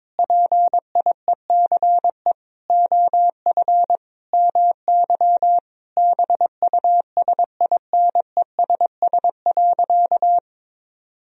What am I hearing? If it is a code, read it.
Morse code, heard as a keyed tone, about 22 wpm, 700 Hz. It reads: PIECE OF MY BUSINESS.
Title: Sentences from Top 1000 Words - Review Part 2 - 22wpm